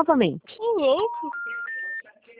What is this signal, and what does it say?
Special information tone, with speech.
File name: out- -2003- - - .346